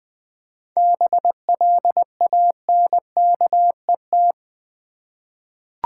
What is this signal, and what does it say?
Morse code, heard as a keyed tone, about 20 wpm, 700 Hz. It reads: BLANKET
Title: 7 Letter Words - First 3 Letters Given - 20wpm